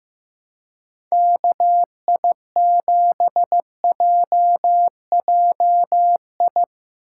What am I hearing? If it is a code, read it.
Morse code, heard as a keyed tone, about 15 wpm, 700 Hz. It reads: KI7JJI
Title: USA Call Signs - 6 Characters - 2x Character Spacing - 15wpm